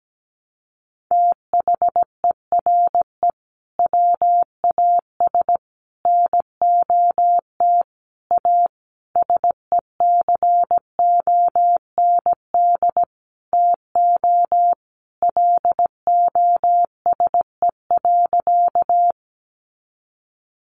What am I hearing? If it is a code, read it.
Morse code, heard as a keyed tone, about 17 wpm, 700 Hz. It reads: THERE WAS NOT A SECOND TO LOSE.
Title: Sentences from Top 1100 Words 17wpm